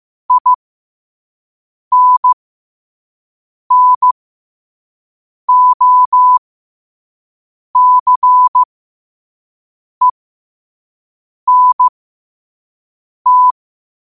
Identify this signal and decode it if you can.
Morse code, heard as a keyed tone, about 15 wpm, spaced as for 5 wpm, 1000 Hz. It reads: INNOCENT
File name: qotdfiles / 5wpm